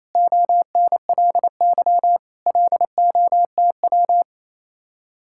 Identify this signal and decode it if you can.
Morse code, heard as a keyed tone, about 33 wpm, 700 Hz. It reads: ONLY LOTW
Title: Speed-Racing Common QSO Phrases - 28wpm